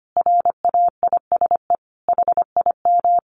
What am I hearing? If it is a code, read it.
Morse code, heard as a keyed tone, about 25 wpm, 700 Hz. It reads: RAISE HIM
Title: Sentences from Top 1400 Words 25wpm